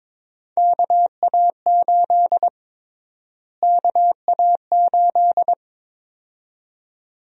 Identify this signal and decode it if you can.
Morse code, heard as a keyed tone, about 22 wpm, 700 Hz. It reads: KA8 KA8
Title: USA Call Sign Prefixes - Repeated 2x - 22wpm